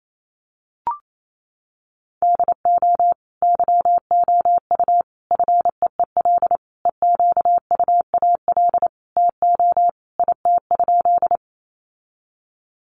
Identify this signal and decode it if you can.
Morse code, heard as a keyed tone, about 28 wpm, 700 Hz. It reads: DO YOU FEEL EQUAL TO IT?
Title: Sentences from Top 1400 Words 28wpm